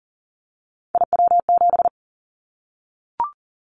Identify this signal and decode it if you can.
Morse code, heard as a keyed tone, about 42 wpm, 700 Hz. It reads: IW7
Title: Callsign Prefixes for Top 25 DXCC - 40wpm